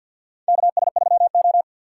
Morse code, heard as a keyed tone, about 51 wpm, 700 Hz. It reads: KS3O
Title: USA Call Signs - 4x Character Spacing - 50wpm